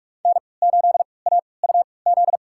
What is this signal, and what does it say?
Morse code, heard as a keyed tone, about 46 wpm, spaced as for 26 wpm, 700 Hz. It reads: N8AUZ
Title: USA Call Signs - 5 Characters - 3x Character Spacing - 45wpm